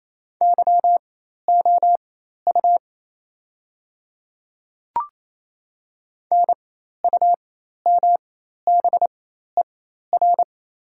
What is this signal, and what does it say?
Morse code, heard as a keyed tone, about 28 wpm, spaced as for 13 wpm, 690 Hz. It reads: YOU NUMBER
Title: 4x Character Spacing - Top 100 Words - 28wpm